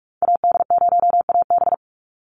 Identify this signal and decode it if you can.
Morse code, heard as a keyed tone, about 44 wpm, 700 Hz. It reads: AD0AB
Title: Call Signs - 45wpm